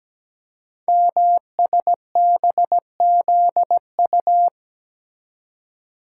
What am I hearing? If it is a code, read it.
Morse code, heard as a keyed tone, about 17 wpm, 700 Hz. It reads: MSBZU
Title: Code Groups - 5 Letters - 17wpm